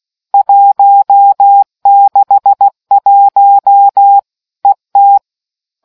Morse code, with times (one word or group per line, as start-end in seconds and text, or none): 0.3-4.2 161
4.7-5.2 ET